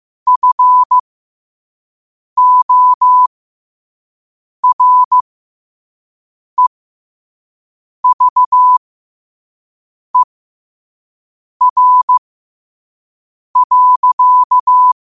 Morse code, heard as a keyed tone, about 15 wpm, spaced as for 5 wpm, 1000 Hz. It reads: FOREVER.
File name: qotdfiles / 5wpm